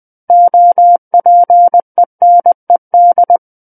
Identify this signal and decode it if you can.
Morse code, heard as a keyed tone, about 20 wpm, 700 Hz. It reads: OPENED
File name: 20 WPM